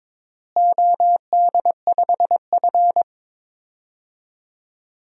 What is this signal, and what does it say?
Morse code, heard as a keyed tone, about 22 wpm, 700 Hz. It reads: OD5F
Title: DXCC-100 Call Signs - 4 Characters - Repeated 3x - 22wpm